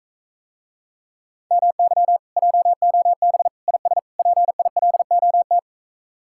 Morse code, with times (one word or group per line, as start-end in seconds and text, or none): 1.5-2.2 MY
2.4-3.5 JOB
3.7-4.0 IS
4.2-5.6 PILOT